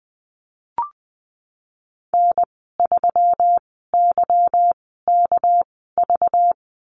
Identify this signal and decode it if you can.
Morse code, heard as a keyed tone, about 19 wpm, spaced as for 14 wpm, 700 Hz. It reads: N3YKV
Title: 200 Lesson 32g - Letter k - Call Signs 20wpm